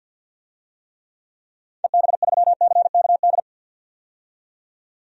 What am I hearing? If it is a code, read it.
Morse code, heard as a keyed tone, about 51 wpm, 700 Hz. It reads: EB3CKD
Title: DXCC-100 Call Signs - 6 Characters - 50wpm